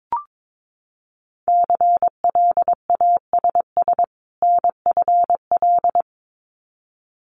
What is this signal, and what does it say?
Morse code, heard as a keyed tone, about 22 wpm, 700 Hz. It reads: CLASS NFL
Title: Winter Field Day Exchanges 22wpm